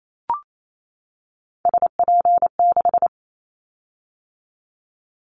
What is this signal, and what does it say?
Morse code, heard as a keyed tone, about 28 wpm, 700 Hz. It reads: SP6
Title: Callsign Prefixes for Top 25 DXCC - 28wpm